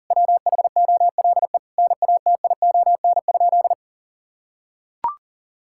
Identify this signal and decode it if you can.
Morse code, heard as a keyed tone, about 38 wpm, 700 Hz. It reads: WHOLE NATION?
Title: Sentences from Top 1000 Words - Review Part 6 - 40wpm